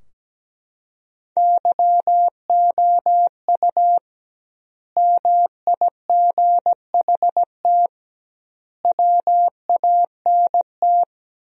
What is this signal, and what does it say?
Morse code, heard as a keyed tone, about 17 wpm, 700 Hz. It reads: YOU MIGHT WANT
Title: 2x Word-Spacing - Sentences from Top 300 Words - 17wpm